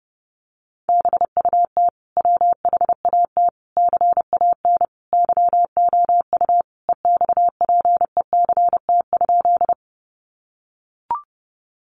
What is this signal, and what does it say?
Morse code, heard as a keyed tone, about 28 wpm, 700 Hz. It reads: BUT WHAT CAN YOU EXPECT?
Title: Sentences from Top 1100 Words 30wpm